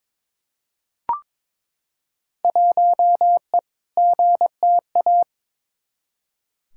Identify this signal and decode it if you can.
Morse code, heard as a keyed tone, about 22 wpm, 700 Hz. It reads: 1E GTA
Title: Field Day Exchanges - 22wpm